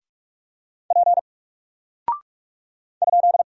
Morse code, heard as a keyed tone, about 44 wpm, 700 Hz. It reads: P ?
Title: Single Character (Letter:Number:Punctuation:Prosign) Alt 45wpm